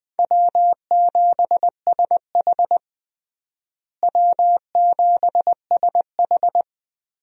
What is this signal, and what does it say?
Morse code, heard as a keyed tone, about 20 wpm, 700 Hz. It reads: W7SH W7SH
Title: USA Call Signs - 4 Characters - Repeated 3x - 20wpm